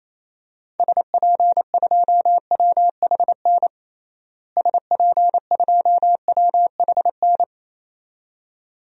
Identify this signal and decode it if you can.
Morse code, heard as a keyed tone, about 28 wpm, 700 Hz. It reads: SP2WHN SP2WHN
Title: DXCC-100 Call Signs - 6 Characters - Repeated 2x - 28wpm